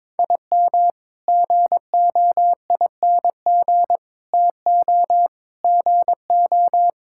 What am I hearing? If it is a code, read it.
Morse code, heard as a keyed tone, about 21 wpm, 700 Hz. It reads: IM GOING TO GO
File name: practice / 22wpm / Sentences with 8-Word Phrases - 22wpm